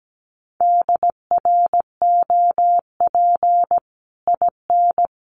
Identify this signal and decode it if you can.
Morse code, heard as a keyed tone, about 17 wpm, 700 Hz. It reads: DROP IN
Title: Sentences from Top 1400 Words 17wpm